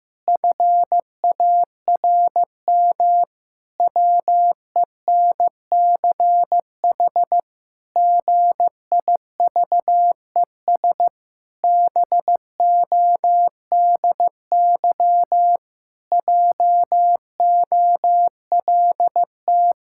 Morse code, heard as a keyed tone, about 15 wpm, 700 Hz. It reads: FARM WENCH GIVES BODY JOLT
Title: Pangram Challenge v2 - 15wpm